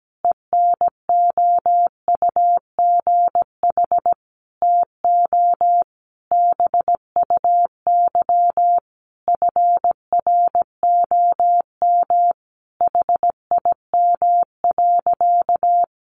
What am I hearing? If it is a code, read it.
Morse code, heard as a keyed tone, about 17 wpm, 700 Hz. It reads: ENOUGH TO BUY FROM HIM.
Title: Sentences from Top 1200 Words 17wpm